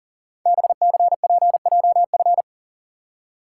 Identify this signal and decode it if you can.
Morse code, heard as a keyed tone, about 38 wpm, 700 Hz. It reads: BCPJF
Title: Code Groups - 5 Letters - 40wpm